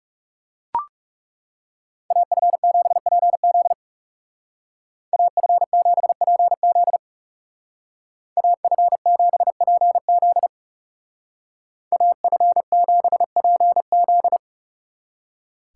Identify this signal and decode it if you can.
Morse code, heard as a keyed tone, about 40 wpm, 700 Hz. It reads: AF7PZ AF7PZ AF7PZ AF7PZ